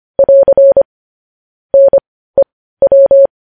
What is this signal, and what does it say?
Morse code, heard as a keyed tone, about 25 wpm, spaced as for 15 wpm, 550 Hz. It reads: <AR> NEW
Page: news headlines in morse code at 15 wpm